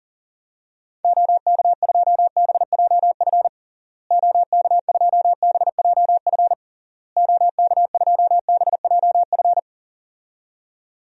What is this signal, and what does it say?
Morse code, heard as a keyed tone, about 40 wpm, 700 Hz. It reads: OK2BJF OK2BJF OK2BJF
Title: DXCC-100 Call Signs - 6 Characters - Repeated 3x - 40wpm